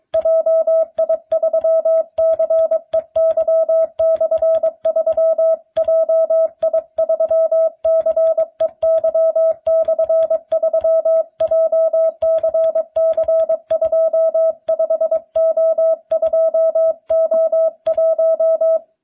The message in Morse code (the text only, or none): JI3CEY/3 JI3CEY/3 JCC25O2O1